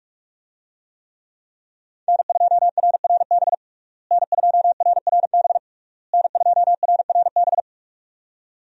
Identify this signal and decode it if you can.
Morse code, heard as a keyed tone, about 46 wpm, 700 Hz. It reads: N2RRB N2RRB N2RRB
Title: USA Call Signs - 5 Characters - Repeated 3x - 45wpm